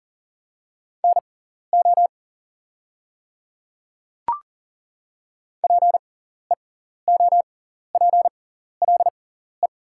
Morse code, heard as a keyed tone, about 39 wpm, spaced as for 14 wpm, 700 Hz. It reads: NO PEOPLE
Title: 6x Character Spacing - Top 200 Words - 40wpm